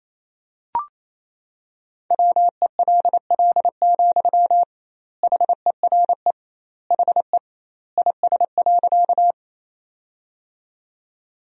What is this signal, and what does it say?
Morse code, heard as a keyed tone, about 27 wpm, 700 Hz. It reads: WELL, HERE HE IS.
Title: 2x Word-Spacing - Sentences from Top 200 Words - 28wpm